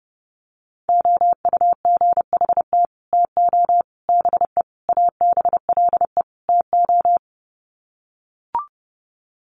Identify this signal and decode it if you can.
Morse code, heard as a keyed tone, about 30 wpm, 700 Hz. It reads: OUGHT TO BE ABLE TO
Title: Sets of 5 Words Encore2 30wpm